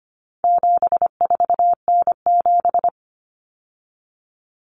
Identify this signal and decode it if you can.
Morse code, heard as a keyed tone, about 25 wpm, 700 Hz. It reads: 74N7
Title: Common Cut Numbers - 4-Digit 25wpm